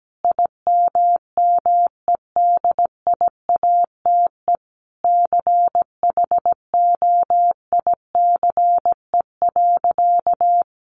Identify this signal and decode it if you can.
Morse code, heard as a keyed tone, about 17 wpm, 700 Hz. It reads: IMMEDIATE CHOICE.
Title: Sentences from Top 1200 Words 17wpm